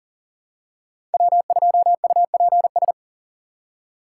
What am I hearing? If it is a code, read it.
Morse code, heard as a keyed tone, about 40 wpm, 700 Hz. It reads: W2UPS
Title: USA Call Signs - 5 Characters - 2x Character Spacing - 40wpm